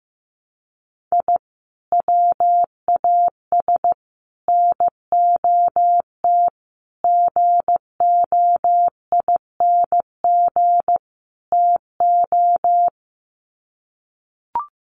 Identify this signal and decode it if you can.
Morse code, heard as a keyed tone, about 15 wpm, 700 Hz. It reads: I WAS NOT GOING TO
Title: Sets of 5 Words 15wpm